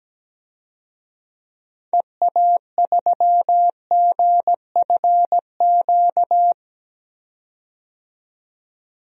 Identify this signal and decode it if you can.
Morse code, heard as a keyed tone, about 17 wpm, 700 Hz. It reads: EA3GFQ
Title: DXCC-100 Call Signs - 6 Characters - 17wpm